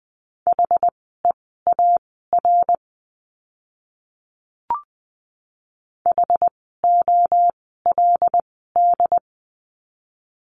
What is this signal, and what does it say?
Morse code, heard as a keyed tone, about 20 wpm, spaced as for 14 wpm, 700 Hz. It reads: HEAR HOLD
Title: 075 Lesson 13f - Letter L - Words 20wpm